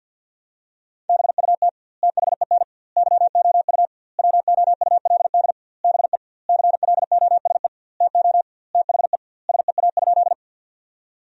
Morse code, heard as a keyed tone, about 50 wpm, 700 Hz. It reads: BUT THEN YOU WOULD BE CLOSE TO THE SEA?
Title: 2x Word-Spacing - Sentences from Top 300 Words - 50wpm